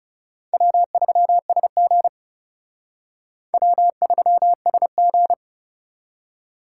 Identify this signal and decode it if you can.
Morse code, heard as a keyed tone, about 37 wpm, 700 Hz. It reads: W3SG W3SG
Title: Speed Racing - USA 1x2 Call Signs 30wpm